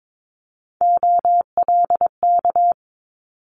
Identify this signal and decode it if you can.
Morse code, heard as a keyed tone, about 22 wpm, 700 Hz. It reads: OLK
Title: Multi-letter Phonemes Quick Fire 22wpm